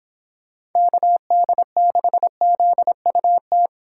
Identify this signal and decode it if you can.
Morse code, heard as a keyed tone, about 26 wpm, 700 Hz. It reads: KD6ZUT
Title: Speed Racing - USA 2x3 Call Signs 22wpm